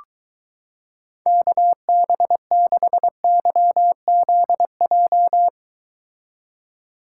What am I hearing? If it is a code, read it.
Morse code, heard as a keyed tone, about 23 wpm, 700 Hz. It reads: KB6YZJ